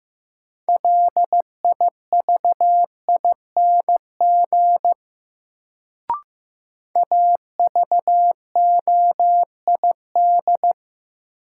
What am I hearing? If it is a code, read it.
Morse code, heard as a keyed tone, about 15 wpm, 700 Hz. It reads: LIVING AVOID
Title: Sentences from Top 1000 Words - Review Part 6 - New Word Drill - 15wpm